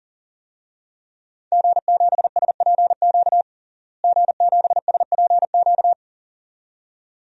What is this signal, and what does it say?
Morse code, heard as a keyed tone, about 40 wpm, 700 Hz. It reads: G7SPQ G7SPQ
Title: DXCC-100 Call Signs - 5 Characters - Repeated 2x - 40wpm